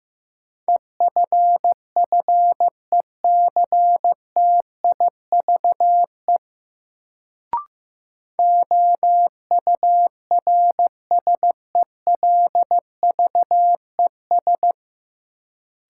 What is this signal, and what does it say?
Morse code, heard as a keyed tone, about 15 wpm, 700 Hz. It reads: EFFECTIVE OURSELVES
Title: Sentences from Top 1600 Words - New Word Drill Alt - 15wpm